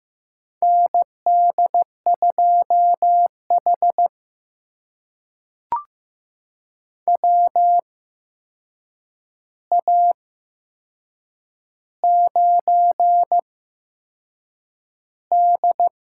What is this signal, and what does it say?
Morse code, heard as a keyed tone, about 15 wpm, spaced as for 6 wpm, 700 Hz. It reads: ND2H WA9D
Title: USA Call Signs - 4 Characters - 8x Character Spacing - 15wpm